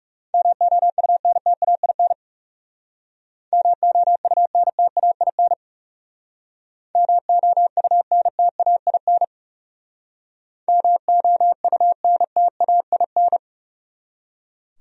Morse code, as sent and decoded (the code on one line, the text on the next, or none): -- --- ..- -. - .- .. -. / -- --- ..- -. - .- .. -. / -- --- ..- -. - .- .. -. / -- --- ..- -. - .- .. -.
MOUNTAIN MOUNTAIN MOUNTAIN MOUNTAIN